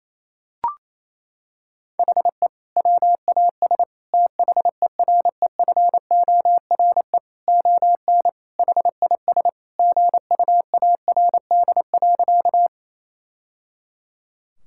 Morse code, heard as a keyed tone, about 27 wpm, 700 Hz. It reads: HE WAS THEREFORE ON HIS GUARD.
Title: Sentences from Top 1000 Words - Review Part 5 - 28wpm